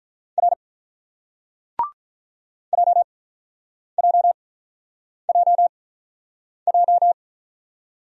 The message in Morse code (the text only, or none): R J J J J